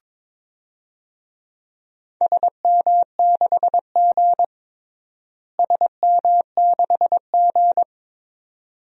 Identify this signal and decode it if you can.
Morse code, heard as a keyed tone, about 22 wpm, 700 Hz. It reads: SM6G SM6G
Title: DXCC-100 Call Signs - 4 Characters - Repeated 2x - 22wpm